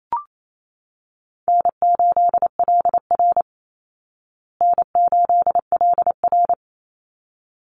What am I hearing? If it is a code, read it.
Morse code, heard as a keyed tone, about 28 wpm, 700 Hz. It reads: N8LR N8LR
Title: Easy Call Signs Repeated 2x - 28wpm